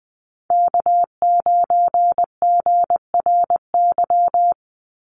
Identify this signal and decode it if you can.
Morse code, heard as a keyed tone, about 20 wpm, 700 Hz. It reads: K9GRY
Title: USA Call Signs 20wpm